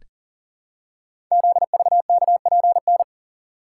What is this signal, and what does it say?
Morse code, heard as a keyed tone, about 41 wpm, 700 Hz. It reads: ZVKPN
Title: Code Groups - 5 Letters - 40wpm